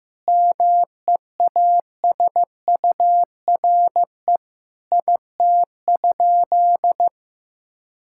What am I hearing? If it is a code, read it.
Morse code, heard as a keyed tone, about 15 wpm, 700 Hz. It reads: MEASURE IT?
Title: Sentences from Top 1000 Words - Review Part 6 - 15wpm